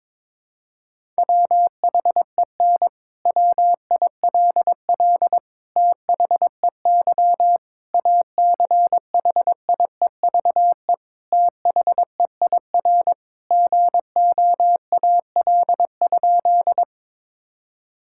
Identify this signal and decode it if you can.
Morse code, heard as a keyed tone, about 22 wpm, 700 Hz. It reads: WHEN WILL THEY ACHIEVE THEIR GOAL?
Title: Sentences from Top 1000 Words - Review Part 6 - 22wpm